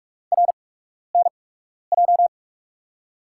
Morse code, heard as a keyed tone, about 44 wpm, spaced as for 12 wpm, 690 Hz. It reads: RNJ
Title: USA Call Signs - 5 Characters - 8x Character Spacing - 45wpm